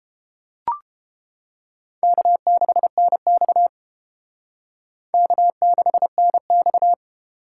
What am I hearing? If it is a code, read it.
Morse code, heard as a keyed tone, about 31 wpm, 700 Hz. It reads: K6NX K6NX